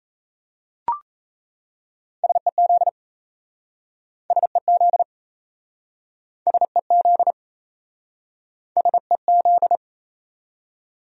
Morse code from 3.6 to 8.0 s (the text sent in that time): SEZ SEZ